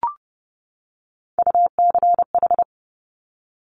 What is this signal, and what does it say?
Morse code, heard as a keyed tone, about 30 wpm, 690 Hz. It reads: UCH
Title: Top Three Letter Combinations in Words 30wpm